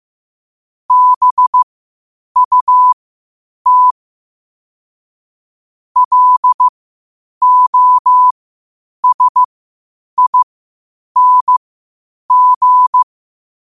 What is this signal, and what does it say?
Morse code, heard as a keyed tone, about 15 wpm, spaced as for 8 wpm, 1000 Hz. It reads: BUT LOSING
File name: qotdfiles / 8wpm